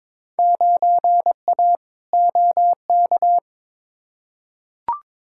Morse code, heard as a keyed tone, about 22 wpm, 700 Hz. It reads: 9A OK
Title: Field Day Exchanges Encore - 22wpm